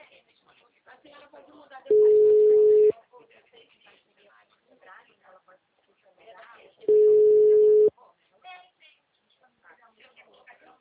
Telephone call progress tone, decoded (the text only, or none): ringback tone